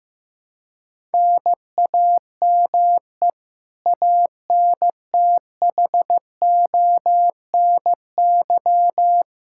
Morse code, heard as a keyed tone, about 15 wpm, 700 Hz. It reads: NAME ANTHONY
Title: Practice QSO - Exchange 1 - Beginning - 15wpm